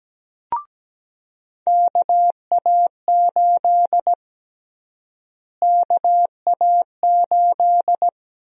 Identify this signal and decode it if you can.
Morse code, heard as a keyed tone, about 17 wpm, 700 Hz. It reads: KA8 KA8
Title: USA Call Sign Prefixes - Repeated 2x - 17wpm